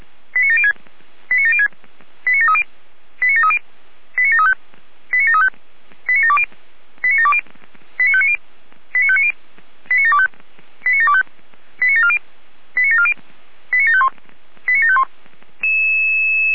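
Five-tone selective call (ZVEI), heard as quick runs of five tones, tones of about 70 ms. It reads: 89706 89706 89730 89730 89735 89735 89720 89720 89590 89590 89725 89725 89740 89740 89751 89751
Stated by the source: ZVEI five-tone call 89752 received with an SDR.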